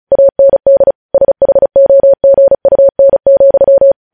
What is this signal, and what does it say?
Morse code, heard as a keyed tone, about 36 wpm, 550 Hz. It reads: AND SHOGUN,
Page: News headlines in Morse code at 35 wpm